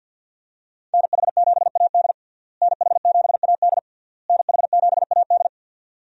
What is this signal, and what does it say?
Morse code, heard as a keyed tone, about 52 wpm, 700 Hz. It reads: NH7AD NH7AD NH7AD
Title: USA Call Signs - 5 Characters - Repeated 3x - 50wpm